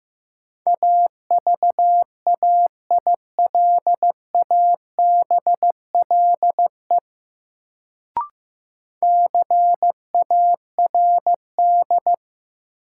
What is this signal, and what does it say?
Morse code, heard as a keyed tone, about 15 wpm, 700 Hz. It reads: AVAILABLE CARD
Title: Sentences from Top 600 Words - New Word Drill - 15wpm